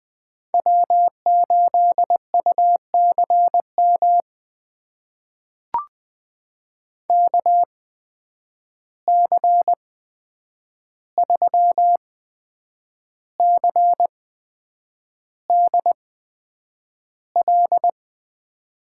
Morse code, heard as a keyed tone, about 20 wpm, spaced as for 8 wpm, 700 Hz. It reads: W8UCM KC3CDL
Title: USA Call Signs - 8x Character Spacing - 20wpm